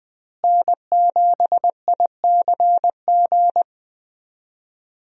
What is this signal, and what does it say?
Morse code, heard as a keyed tone, about 20 wpm, 700 Hz. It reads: N7ICG
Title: USA Call Signs - 5 Characters - Repeated 2x - 20wpm